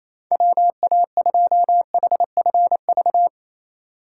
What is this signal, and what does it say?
Morse code, heard as a keyed tone, about 28 wpm, 700 Hz. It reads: WA2HFV
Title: USA Call Signs - 6 Characters - 4x Character Spacing - 28wpm